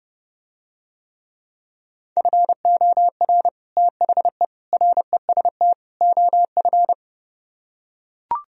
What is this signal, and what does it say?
Morse code, heard as a keyed tone, about 32 wpm, 700 Hz. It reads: FOR THE REST OF